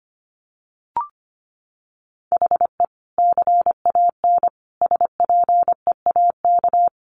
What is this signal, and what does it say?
Morse code, heard as a keyed tone, about 25 wpm, 700 Hz. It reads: HE CAN SPEAK
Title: Sentences from Top 1100 Words 25wpm